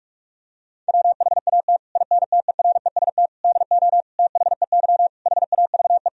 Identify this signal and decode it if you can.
Morse code, heard as a keyed tone, about 41 wpm, 700 Hz. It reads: WHAT INTEREST DO THEY HAVE
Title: Sentences from Top 1000 Words - Review Part 2 - 45wpm